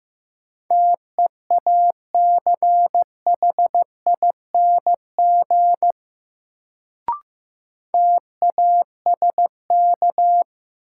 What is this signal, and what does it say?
Morse code, heard as a keyed tone, about 15 wpm, 700 Hz. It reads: TEACHING TASK
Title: Sentences from Top 1100 Words - New Word Drill Alt - 15wpm